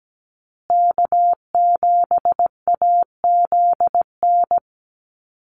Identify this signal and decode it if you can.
Morse code, heard as a keyed tone, about 17 wpm, 700 Hz. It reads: K7AZN
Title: USA Call Signs Repeated 2x 17wpm